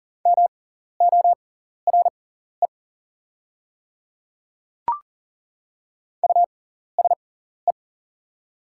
Morse code, heard as a keyed tone, about 40 wpm, spaced as for 14 wpm, 690 Hz. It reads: MORE USE
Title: 6x Character Spacing - Top 200 Words - 40wpm